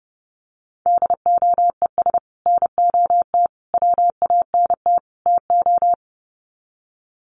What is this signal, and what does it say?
Morse code, heard as a keyed tone, about 30 wpm, 700 Hz. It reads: DOES NOT WANT TO
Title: Sets of 4 Words 30wpm